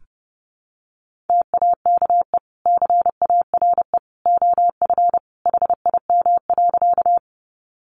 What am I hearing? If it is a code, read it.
Morse code, heard as a keyed tone, about 30 wpm, 700 Hz. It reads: TAKE CARE OF HIM.
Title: Sentences from Top 300 Words 30wpm